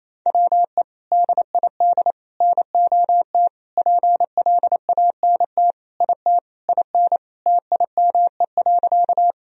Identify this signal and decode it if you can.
Morse code, heard as a keyed tone, about 28 wpm, 700 Hz. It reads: WE DID NOT PLANT IT IN TIME.
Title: Sentences from Top 1000 Words - Review Part 1 - 28wpm